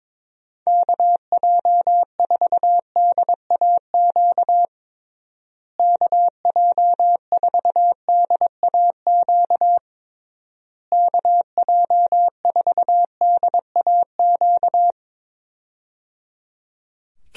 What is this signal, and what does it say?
Morse code, heard as a keyed tone, about 22 wpm, 700 Hz. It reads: KJ4DAQ KJ4DAQ KJ4DAQ
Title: USA Call Signs - 6 Characters - Repeated 3x - 22wpm